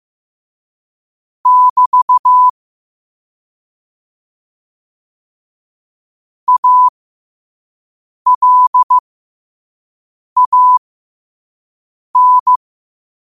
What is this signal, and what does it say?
Morse code, heard as a keyed tone, about 15 wpm, spaced as for 5 wpm, 1000 Hz. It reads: <BT> ALAN
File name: qotdfiles / 5wpm